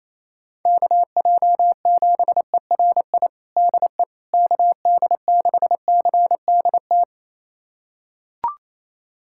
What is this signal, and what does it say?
Morse code, heard as a keyed tone, about 28 wpm, 700 Hz. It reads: KJ7ERI DE KD6CDT
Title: USA Callsign DE Callsign 28wpm